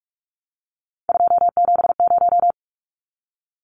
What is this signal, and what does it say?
Morse code, heard as a keyed tone, about 44 wpm, 700 Hz. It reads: 270
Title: Numbers - 3-Digit 45wpm